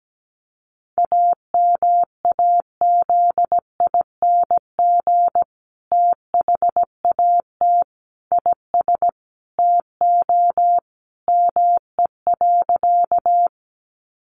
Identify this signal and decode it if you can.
Morse code, heard as a keyed tone, about 17 wpm, 700 Hz. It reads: AMAZING THAT IS TO ME.
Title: Sentences from Top 1500 Words 17wpm